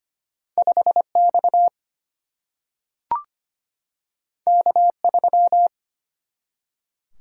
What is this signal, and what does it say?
Morse code, heard as a keyed tone, about 25 wpm, 700 Hz. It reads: 5X K3
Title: Single-Letter Single-Number 25wpm